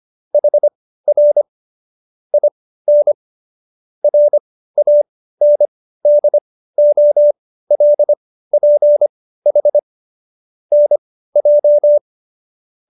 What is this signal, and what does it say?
Morse code, heard as a keyed tone, about 25 wpm, spaced as for 15 wpm, 590 Hz. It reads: HR IN RANDOLPH NJ
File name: qso220b 15